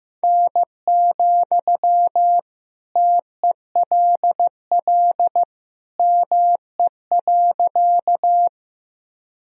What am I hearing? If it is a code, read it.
Morse code, heard as a keyed tone, about 15 wpm, 700 Hz. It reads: N, TELL ME.
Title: Sentences from Top 200 Words and Abbreviations 15wpm